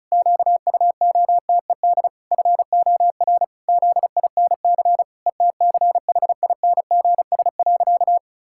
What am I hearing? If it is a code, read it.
Morse code, heard as a keyed tone, about 36 wpm, 700 Hz. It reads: QUOTED FOR ZINC ETCHINGS.
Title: Pangram Challenge v2 - 35wpm